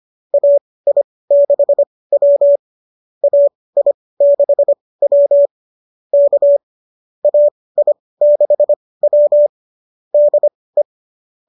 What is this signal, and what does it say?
Morse code, heard as a keyed tone, about 25 wpm, spaced as for 18 wpm, 590 Hz. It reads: AI6W AI6W K AI6W DE
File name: qso207 18